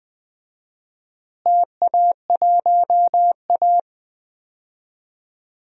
Morse code, heard as a keyed tone, about 20 wpm, 700 Hz. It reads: TA1A
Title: DXCC-100 Call Signs - 4 Characters - Repeated 3x - 20wpm